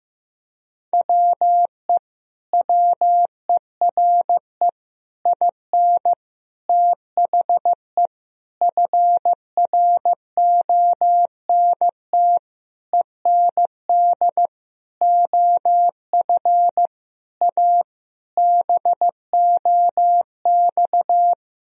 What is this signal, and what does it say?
Morse code, heard as a keyed tone, about 16 wpm, 700 Hz. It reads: WE WERE IN THE FRONT END OF A BOX
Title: Sentences from Top 1000 Words - Review Part 2 - 15wpm